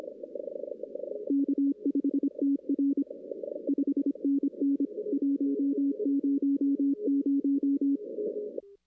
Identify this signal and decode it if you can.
Morse code, heard as a keyed tone, about 24 wpm, 280 Hz. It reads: K5TR 5NN 100
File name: 2016 arrl dx cw / 2016-ARRL-DX-CW-K5TR- - 0-26